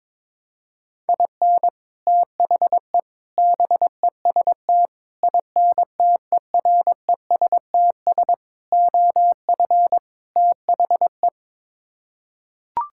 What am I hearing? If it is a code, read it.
Morse code, heard as a keyed tone, about 22 wpm, 700 Hz. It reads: IN THE BEST INTERESTS OF THE
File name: practice / 22wpm / Sets of 6 Words - 22wpm